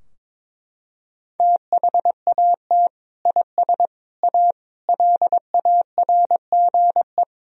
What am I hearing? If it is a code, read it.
Morse code, heard as a keyed tone, about 22 wpm, 700 Hz. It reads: THAT IS A LARGE
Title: Sentences from Top 1000 Words - Review Part 2 - 22wpm